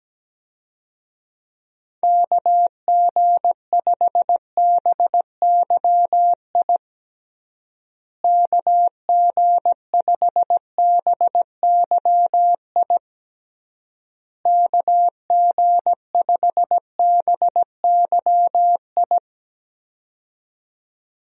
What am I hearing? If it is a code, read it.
Morse code, heard as a keyed tone, about 17 wpm, 700 Hz. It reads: KG5BYI KG5BYI KG5BYI
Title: USA Call Signs - 6 Characters - Repeated 3x - 17wpm